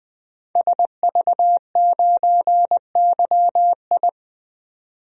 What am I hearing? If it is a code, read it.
Morse code, heard as a keyed tone, about 20 wpm, 700 Hz. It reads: SV9YI
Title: DXCC-100 Call Signs - 5 Characters - 20wpm